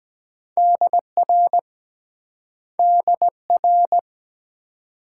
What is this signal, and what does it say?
Morse code, heard as a keyed tone, about 23 wpm, 690 Hz. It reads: DR DR